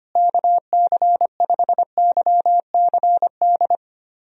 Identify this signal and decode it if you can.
Morse code, heard as a keyed tone, about 25 wpm, 700 Hz. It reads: KC5YCD
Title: USA Call Signs - 6 Characters - Repeated 2x - 25wpm